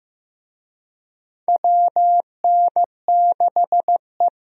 Morse code, heard as a keyed tone, about 15 wpm, 700 Hz. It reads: WN6E
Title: USA Call Signs - 4 Characters - Repeated 2x - 15wpm